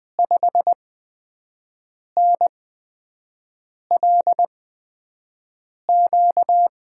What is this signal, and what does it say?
Morse code, heard as a keyed tone, about 20 wpm, spaced as for 5 wpm, 700 Hz. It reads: 5NLQ
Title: USA Call Signs - 5 Characters - 8x Character Spacing - 20wpm